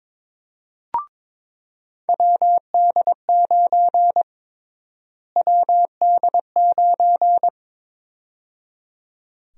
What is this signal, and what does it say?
Morse code, heard as a keyed tone, about 21 wpm, 700 Hz. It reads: WD9 WD9
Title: USA Call Sign Prefixes - Repeated 2x - 22wpm